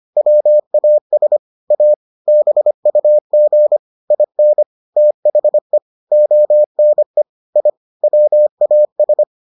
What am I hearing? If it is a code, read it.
Morse code, heard as a keyed tone, about 25 wpm, 600 Hz. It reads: WAS A BUG IN THE ONE I WAS